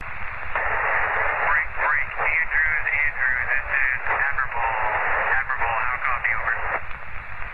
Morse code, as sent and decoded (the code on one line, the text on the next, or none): none